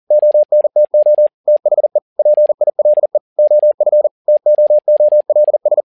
Morse code, heard as a keyed tone, about 40 wpm, 600 Hz. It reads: ONTO THE PILE OF TOOLS
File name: learning center / tower trouble / tower touble 40wpm